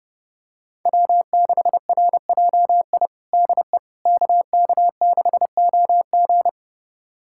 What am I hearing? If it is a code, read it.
Morse code, heard as a keyed tone, about 32 wpm, 700 Hz. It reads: W6RJI DE KK6OG
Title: USA Callsign DE Callsign - 5 Characters - 30wpm